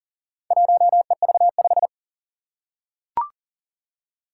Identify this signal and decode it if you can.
Morse code, heard as a keyed tone, about 40 wpm, 700 Hz. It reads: 1EV5
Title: Four Challenging Letter-Number Combinations 40wpm